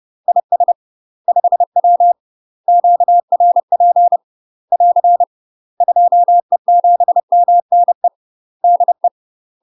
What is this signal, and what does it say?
Morse code, heard as a keyed tone, about 30 wpm, 700 Hz. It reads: IS 5W QRP <AR> 2E7MNE DE
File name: UK MASTER QSOs TRAINING / 30WPM / UK MASTER QSOs 0036